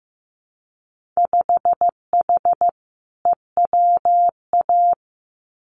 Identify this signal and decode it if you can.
Morse code, heard as a keyed tone, about 15 wpm, 700 Hz. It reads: 5H EWA
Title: Winter Field Day Exchanges Encore 15wpm